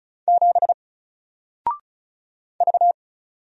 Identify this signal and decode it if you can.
Morse code, heard as a keyed tone, about 36 wpm, 700 Hz. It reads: 7 V
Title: Single Character (Letter:Number:Punctuation:Prosign) 35wpm